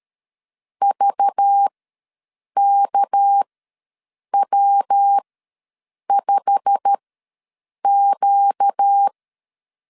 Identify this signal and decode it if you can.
Morse code, heard as a keyed tone, about 13 wpm, spaced as for 7 wpm, 800 Hz. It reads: VKW5Q
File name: LEZIONI / 9 2 ws 12wpm 3406